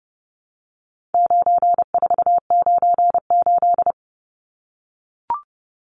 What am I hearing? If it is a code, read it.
Morse code, heard as a keyed tone, about 30 wpm, 700 Hz. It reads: 9498
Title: Numbers - 4-Digit 30wpm